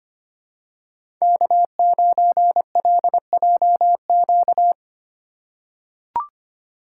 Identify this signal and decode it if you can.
Morse code, heard as a keyed tone, about 25 wpm, 700 Hz. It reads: K9LJQ
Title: USA Call Signs - 5 Characters - 4x Character Spacing - 25wpm